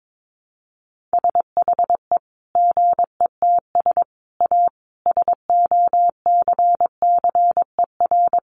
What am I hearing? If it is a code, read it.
Morse code, heard as a keyed tone, about 22 wpm, 700 Hz. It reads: SHE GETS A SOCCER
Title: Easy Phrases 22wpm